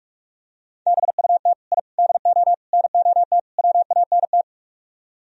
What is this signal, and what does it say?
Morse code, heard as a keyed tone, about 44 wpm, 700 Hz. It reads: BUT I DO NOT WANT